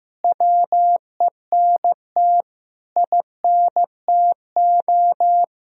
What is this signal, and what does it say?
Morse code, heard as a keyed tone, about 15 wpm, 700 Hz. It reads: WENT INTO
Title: Sentences from Top 1000 Words - Review Part 1 - 15wpm